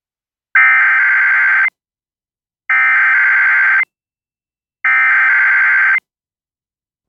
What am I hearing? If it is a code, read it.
Emergency Alert System (SAME) tones (header bursts), heard as harsh buzzing bursts. It is heard from 0.6 to 6.0 s.